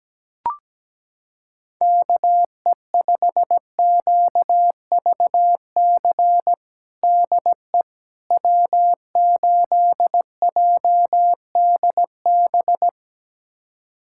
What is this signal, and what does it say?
Morse code, heard as a keyed tone, about 17 wpm, 700 Hz. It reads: KE5QVC DE W8JDB
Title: USA Callsign DE Callsign 17wpm